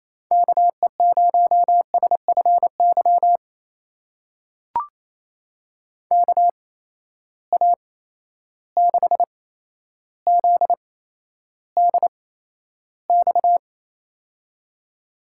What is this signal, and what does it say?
Morse code, heard as a keyed tone, about 28 wpm, spaced as for 12 wpm, 700 Hz. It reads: KE0SFY KA6ZDX
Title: USA Call Signs - 6 Characters - 8x Character Spacing - 28wpm